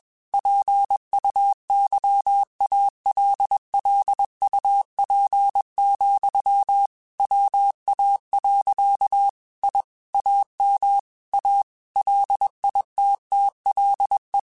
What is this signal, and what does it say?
Morse code, heard as a keyed tone, about 21 wpm, 800 Hz. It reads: PUYALLUP, WA. I AM A LITTLE
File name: QSOs / QSOs-mp3s / kc7haa@20